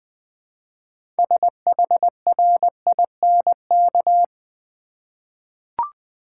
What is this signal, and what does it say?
Morse code, heard as a keyed tone, about 20 wpm, 700 Hz. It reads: SHRINK
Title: 6 Letter Words - First 3 Letters Given - 20wpm